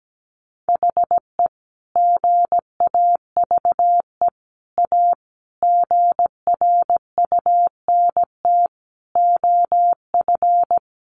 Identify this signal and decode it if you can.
Morse code, heard as a keyed tone, about 17 wpm, 700 Hz. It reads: HE GAVE A GRUNT OF
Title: A Sound Of Thunder 17wpm